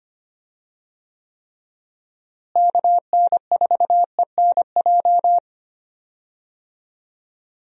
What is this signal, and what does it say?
Morse code, heard as a keyed tone, about 25 wpm, 700 Hz. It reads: KN4ENJ
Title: USA Call Signs - 6 Characters - Repeated 2x - 25wpm